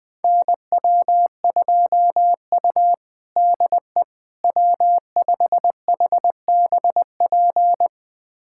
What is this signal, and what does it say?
Morse code, heard as a keyed tone, about 20 wpm, 700 Hz. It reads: NW2U DE W5HBP
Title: USA Callsign DE Callsign 20wpm